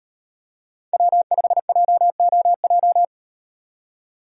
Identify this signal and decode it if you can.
Morse code, heard as a keyed tone, about 35 wpm, 700 Hz. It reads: W5JOJ